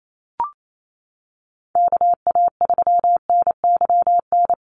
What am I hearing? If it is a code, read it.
Morse code, heard as a keyed tone, about 28 wpm, 700 Hz. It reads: KA3NYN
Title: USA Call Signs 28wpm